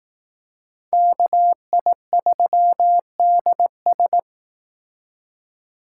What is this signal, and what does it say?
Morse code, heard as a keyed tone, about 19 wpm, 700 Hz. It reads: KI3DS
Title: Speed Racing - USA 2x2 Call Signs 15wpm